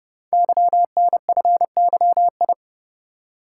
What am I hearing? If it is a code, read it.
Morse code, heard as a keyed tone, about 30 wpm, 700 Hz. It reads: YNFYI
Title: Code Groups - 5 Letters - 30wpm